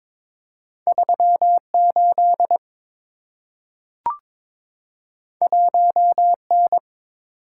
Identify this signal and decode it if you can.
Morse code, heard as a keyed tone, about 22 wpm, 700 Hz. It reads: 38 1N
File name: practice / 22wpm / Two Letter-Number Combinations - 22wpm